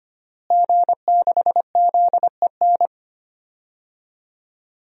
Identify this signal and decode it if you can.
Morse code, heard as a keyed tone, about 25 wpm, 700 Hz. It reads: G6ZEN
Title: DXCC-100 Call Signs - 5 Characters - Repeated 3x - 25wpm